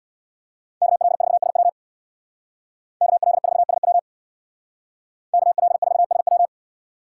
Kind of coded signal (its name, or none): Morse code